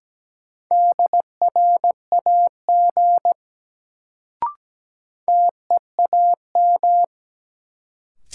Morse code, heard as a keyed tone, about 17 wpm, 700 Hz. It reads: DRAG TEAM
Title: Sentences from Top 1500 Words - New Word Drill Alt - 17wpm